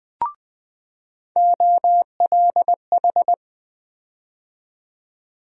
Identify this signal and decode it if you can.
Morse code, heard as a keyed tone, about 21 wpm, 700 Hz. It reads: OLH
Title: USA Call Sign Postfixes - 3 Letters 20wpm